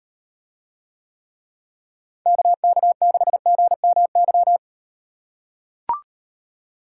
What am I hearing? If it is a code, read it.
Morse code, heard as a keyed tone, about 38 wpm, 700 Hz. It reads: KK6GMY